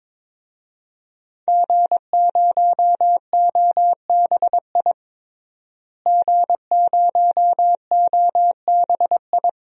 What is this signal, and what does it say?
Morse code, heard as a keyed tone, about 22 wpm, 700 Hz. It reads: G0OBI G0OBI
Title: DXCC-100 Call Signs - 5 Characters - Repeated 2x - 22wpm